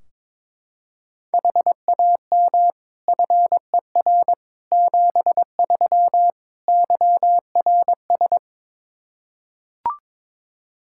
Morse code, heard as a keyed tone, about 21 wpm, 700 Hz. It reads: HAM FER 73 YRS
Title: Practice QSO - Exchange 3 - More Info - 22wpm